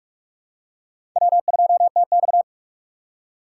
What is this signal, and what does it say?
Morse code, heard as a keyed tone, about 46 wpm, 700 Hz. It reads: W2TX
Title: USA Call Signs - 4 Characters - 4x Character Spacing - 45wpm